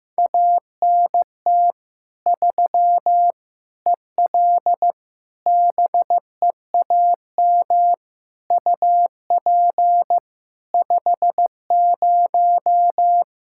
Morse code, heard as a keyed tone, about 16 wpm, 700 Hz. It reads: ANT 3 EL BEAM UP 50
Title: Practice QSO - Exchange 2 - Info - 15wpm